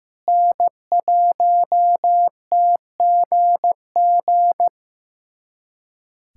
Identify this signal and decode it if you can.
Morse code, heard as a keyed tone, about 14 wpm, 700 Hz. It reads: N1TGG
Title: USA Callsign DE Callsign - 5 Characters - 15wpm